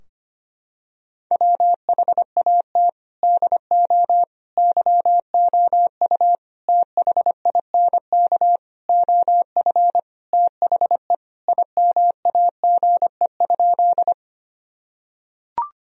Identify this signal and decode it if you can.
Morse code, heard as a keyed tone, about 25 wpm, 700 Hz. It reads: WHAT DO YOU THINK OF THE IMAGE?
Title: Sentences from Top 1000 Words - Review Part 4 - 25wpm